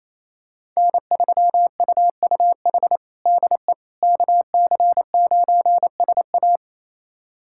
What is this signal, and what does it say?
Morse code, heard as a keyed tone, about 28 wpm, 700 Hz. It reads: N3UUH DE KC9SA
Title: USA Callsign DE Callsign - 5 Characters - 28wpm